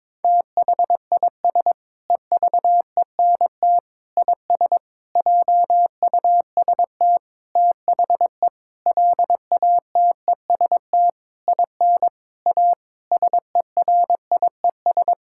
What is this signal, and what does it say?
Morse code, heard as a keyed tone, about 22 wpm, 700 Hz. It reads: THIS EVENT IS JUST THE LATEST IN A SERIES
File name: practice / 22wpm / Sentences with 8-Word Phrases - 22wpm